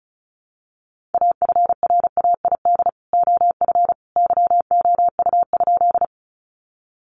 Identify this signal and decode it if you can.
Morse code, heard as a keyed tone, about 34 wpm, 700 Hz. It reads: AFRAID OF YOU?
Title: Sentences from Top 1500 Words 35wpm